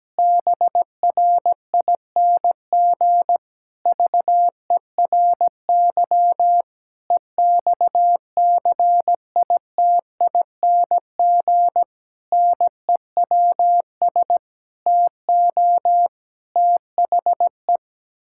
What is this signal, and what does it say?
Morse code, heard as a keyed tone, about 17 wpm, 700 Hz. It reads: BRING VERY EXCITING NEWS TO THE
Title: Pangram Challenge v2 - 17wpm